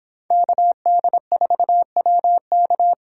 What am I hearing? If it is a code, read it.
Morse code, heard as a keyed tone, about 30 wpm, 700 Hz. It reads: KD4WK